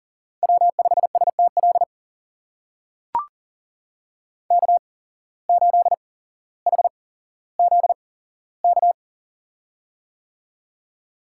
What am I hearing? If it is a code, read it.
Morse code, heard as a keyed tone, about 38 wpm, spaced as for 17 wpm, 700 Hz. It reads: W5STL K8HZK
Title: USA Call Signs - 5 Characters - 8x Character Spacing - 40wpm